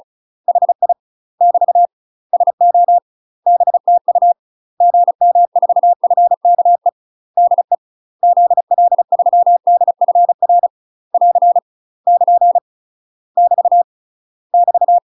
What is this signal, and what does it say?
Morse code, heard as a keyed tone, about 34 wpm, 700 Hz. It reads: HI <BT> SO BTU GM4FKE DE ZL3DFR <AR> <KN> <BT> <BT>
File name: UK MASTER QSOs TRAINING / 35WPM / UK MASTER QSOs 0017